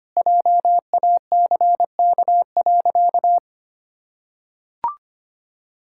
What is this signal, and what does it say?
Morse code, heard as a keyed tone, about 25 wpm, 700 Hz. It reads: JACK.
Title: Pangram Challenge v2 - 25wpm